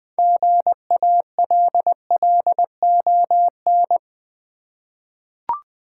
Morse code, heard as a keyed tone, about 20 wpm, 700 Hz. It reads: GALLON
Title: 6 Letter Words - First 3 Letters Given - 20wpm